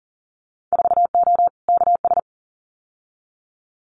Morse code, heard as a keyed tone, about 38 wpm, 700 Hz. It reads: 4O KS
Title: Winter Field Day Exchanges Encore 40wpm